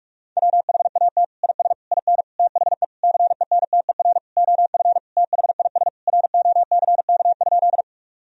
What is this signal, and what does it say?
Morse code, heard as a keyed tone, about 42 wpm, 700 Hz. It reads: WHAT IS IN THE CENTER OF THIS ROCK?
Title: Sentences from Top 1000 Words - Review Part 2 - 45wpm